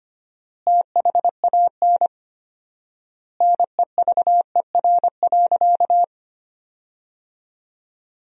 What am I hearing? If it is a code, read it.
Morse code, heard as a keyed tone, about 25 wpm, 700 Hz. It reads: THAN NEVER.
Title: 4x Word-Spacing - Sentences from Top 400 Words - 25wpm